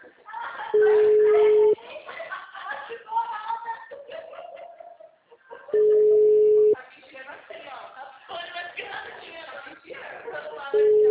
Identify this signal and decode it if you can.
Ringback tone (Europe, 425 Hz), with speech.